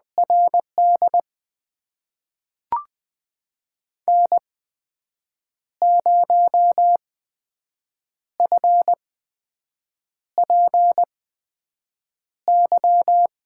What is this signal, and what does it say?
Morse code, heard as a keyed tone, about 20 wpm, spaced as for 6 wpm, 700 Hz. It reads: RD N0FPY
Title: USA Call Signs - 5 Characters - 8x Character Spacing - 20wpm